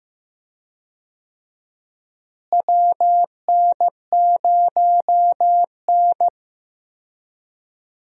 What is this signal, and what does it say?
Morse code, heard as a keyed tone, about 15 wpm, 700 Hz. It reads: WN0N
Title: USA Call Signs - 4 Characters - Repeated 2x - 15wpm